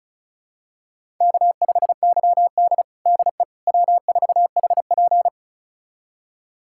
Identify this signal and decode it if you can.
Morse code, heard as a keyed tone, about 36 wpm, 700 Hz. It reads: K5YD DE W4HP
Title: USA Callsign DE Callsign - 4 Characters - 35wpm